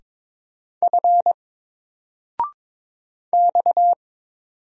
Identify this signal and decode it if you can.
Morse code, heard as a keyed tone, about 22 wpm, 700 Hz. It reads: F X
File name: Single-Letter-Number-22wpm